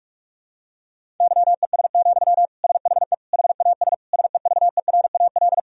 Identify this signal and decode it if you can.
Morse code, heard as a keyed tone, about 46 wpm, 700 Hz. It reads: YES, SHE HAS SEVERAL
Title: Sentences from Top 1000 Words - Review Part 2 - 45wpm